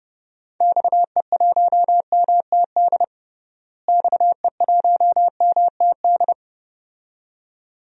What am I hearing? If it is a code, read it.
Morse code, heard as a keyed tone, about 30 wpm, 700 Hz. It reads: XE1MTD XE1MTD
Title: DXCC-100 Call Signs - 6 Characters - Repeated 2x - 30wpm